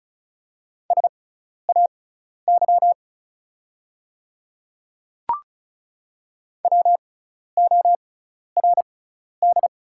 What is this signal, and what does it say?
Morse code, heard as a keyed tone, about 35 wpm, spaced as for 12 wpm, 690 Hz. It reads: SAY WORD
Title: 6x Character Spacing - Top 200 Words - 35wpm